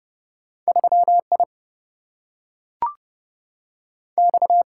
Morse code, heard as a keyed tone, about 30 wpm, 700 Hz. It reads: 3I X
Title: Single-Letter Single-Number 30wpm